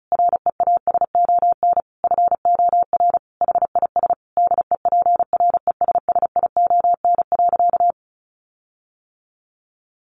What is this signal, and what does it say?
Morse code, heard as a keyed tone, about 34 wpm, 700 Hz. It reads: REASON FOR HIS DEPRESSION.
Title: Sentences from Top 1300 Words 35wpm